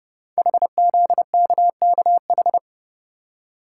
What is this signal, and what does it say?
Morse code, heard as a keyed tone, about 30 wpm, 700 Hz. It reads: HZKKH
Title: Code Groups - 5 Letters - 30wpm